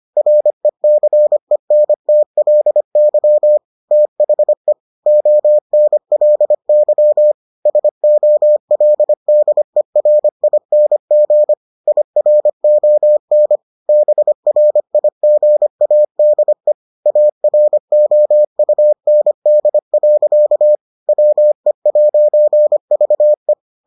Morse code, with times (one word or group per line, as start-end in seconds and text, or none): 0.2-3.6 RECENTLY
3.9-4.7 THE
5.1-7.3 ONLY
7.7-11.5 SOLDERING
11.9-13.6 IRON
13.9-16.7 BRIGADE
17.1-20.8 AROUND.
21.1-23.5 WE'VE